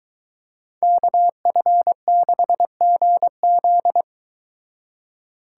Morse code, heard as a keyed tone, about 23 wpm, 700 Hz. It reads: KF6GZ